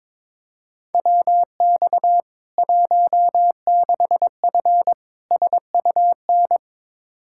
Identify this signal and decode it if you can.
Morse code, heard as a keyed tone, about 22 wpm, 700 Hz. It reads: WX 16F SUN
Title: K1USN SST QSO Practice 22wpm